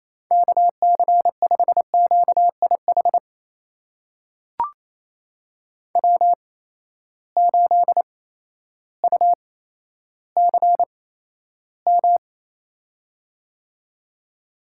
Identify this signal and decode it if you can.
Morse code, heard as a keyed tone, about 28 wpm, spaced as for 13 wpm, 700 Hz. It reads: KC5QIH W8UCM
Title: USA Call Signs - 8x Character Spacing - 28wpm